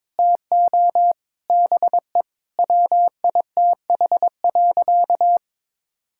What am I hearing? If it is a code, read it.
Morse code, heard as a keyed tone, about 22 wpm, 700 Hz. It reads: TO BE WITH.
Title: Sentences from Top 1000 Words - Review Part 5 - 22wpm